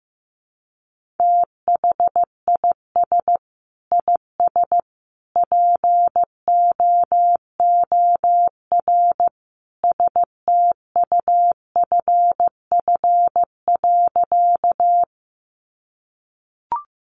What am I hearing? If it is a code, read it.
Morse code, heard as a keyed tone, about 15 wpm, 700 Hz. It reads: THIS IS POOR STUFF.
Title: Sentences from Top 1400 Words 15wpm